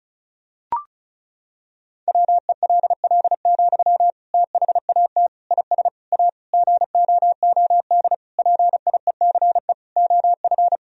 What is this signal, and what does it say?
Morse code, heard as a keyed tone, about 36 wpm, 700 Hz. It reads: WELL, THAT IS A GOOD PIECE OF
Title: Sentences from Top 1000 Words - Review Part 2 - 35wpm